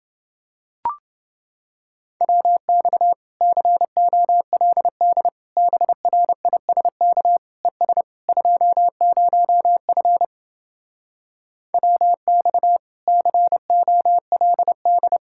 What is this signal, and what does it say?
Morse code, heard as a keyed tone, about 30 wpm, 700 Hz. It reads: WX COLD BRISK ES 20F WX COLD